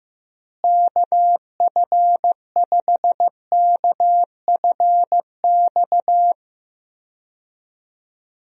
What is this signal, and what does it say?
Morse code, heard as a keyed tone, about 15 wpm, 700 Hz. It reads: KF5KFX
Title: USA Call Signs - 6 Characters - Repeated 3x - 15wpm